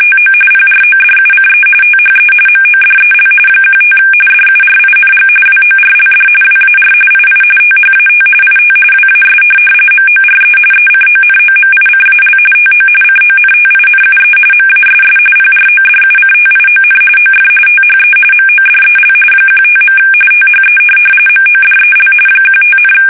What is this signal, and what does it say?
RTTY, about 45.45 baud. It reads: TO PRODUCE THE MODEL 15 DURING WORLD WAR II THAT IT WAS MORE ECONOMICAL TO CONTINUE MASS PRODUCTION OF THE MODEL 15. THE MODEL 15,
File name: fsk850 / teletype-corp